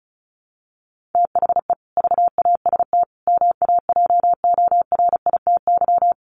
Morse code, heard as a keyed tone, about 34 wpm, 700 Hz. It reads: THE VAST MAJORITY
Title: Sets of 4 Words 35wpm